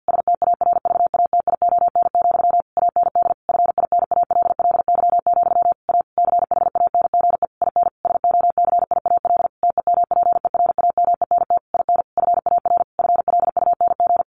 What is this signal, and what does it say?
Morse code, heard as a keyed tone, about 48 wpm, 700 Hz. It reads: STARVATION, AND FINALLY, A CHANGE IN SOCIAL TEMPERAMENT IN FAR FLUNG
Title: A Sound Of Thunder 50wpm